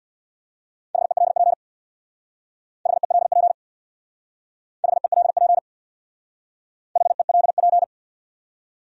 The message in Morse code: HELP HELP HELP HELP

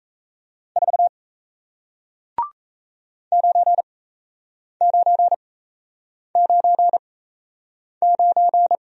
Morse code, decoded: 4 9999

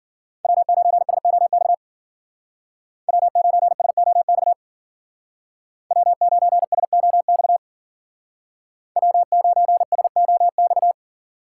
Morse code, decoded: W9SOX W9SOX W9SOX W9SOX

